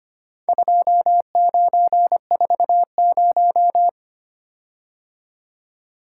Morse code, heard as a keyed tone, about 25 wpm, 700 Hz. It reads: 2940
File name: practice / 25wpm / Four Letter-Number Combinations - 25wpm